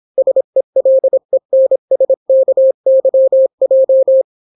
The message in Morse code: SELENSKYJ